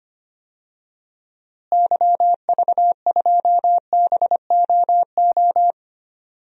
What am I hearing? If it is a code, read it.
Morse code, heard as a keyed tone, about 25 wpm, 700 Hz. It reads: YV2BOO
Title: DXCC-100 Call Signs - 6 Characters - Repeated 2x - 25wpm